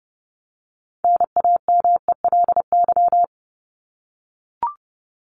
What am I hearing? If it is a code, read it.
Morse code, heard as a keyed tone, about 30 wpm, 700 Hz. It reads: NAMELY
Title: LY Suffix - Top 100 Words with Base Words 30wpm